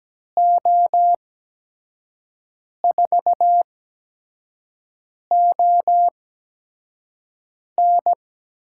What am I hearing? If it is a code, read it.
Morse code, heard as a keyed tone, about 17 wpm, spaced as for 5 wpm, 700 Hz. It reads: O4ON
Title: USA Call Signs - 8x Character Spacing - 17wpm